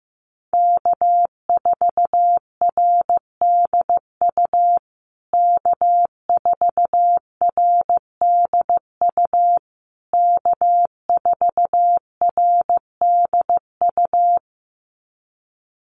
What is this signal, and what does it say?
Morse code, heard as a keyed tone, about 15 wpm, 700 Hz. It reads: K4RDU K4RDU K4RDU
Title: USA Call Signs Repeated 3x 15wpm